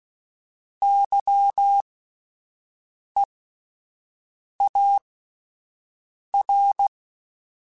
Morse code, heard as a keyed tone, about 16 wpm, spaced as for 5 wpm, 800 Hz. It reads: YEAR
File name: Alphabetical / Alphabetical mp3s / Ys @5